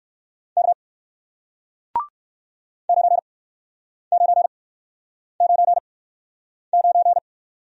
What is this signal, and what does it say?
Morse code, heard as a keyed tone, about 61 wpm, spaced as for 9 wpm, 700 Hz. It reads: K 9999